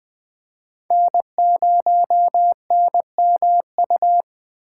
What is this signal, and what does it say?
Morse code, heard as a keyed tone, about 20 wpm, 700 Hz. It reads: N0NMU
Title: USA Call Signs - 8x Character Spacing - 20wpm